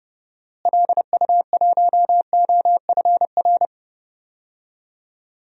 Morse code, heard as a keyed tone, about 30 wpm, 700 Hz. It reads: LU1OFR
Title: DXCC-100 Call Signs - 6 Characters - Repeated 3x - 30wpm